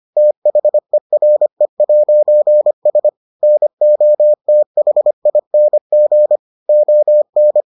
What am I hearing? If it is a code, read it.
Morse code, heard as a keyed tone, about 25 wpm, 600 Hz. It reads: THERE'S NOTHING ON